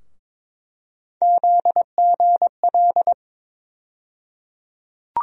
Morse code, heard as a keyed tone, about 22 wpm, 700 Hz. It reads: ZGL
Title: USA Call Sign Postfixes - 3 Letters 22wpm